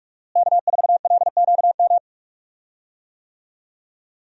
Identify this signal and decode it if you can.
Morse code, heard as a keyed tone, about 46 wpm, 700 Hz. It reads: K4LQM
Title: USA Call Signs - 5 Characters - Repeated 3x - 45wpm